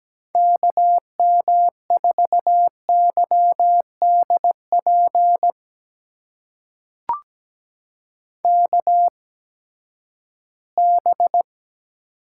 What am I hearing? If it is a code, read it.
Morse code, heard as a keyed tone, about 17 wpm, spaced as for 12 wpm, 700 Hz. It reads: KM4YDP KB
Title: USA Call Signs - 6 Characters - 8x Character Spacing - 17wpm